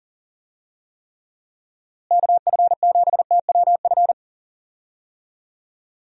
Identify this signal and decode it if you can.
Morse code, heard as a keyed tone, about 40 wpm, 700 Hz. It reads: KF7TWF
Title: USA Call Signs - 6 Characters - Repeated 3x - 40wpm